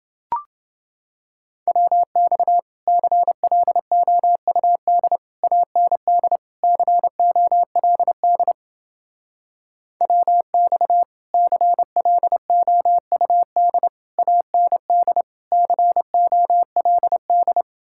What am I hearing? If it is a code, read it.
Morse code, heard as a keyed tone, about 29 wpm, 700 Hz. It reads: WX CLOUD AND COLD WX CLOUD AND COLD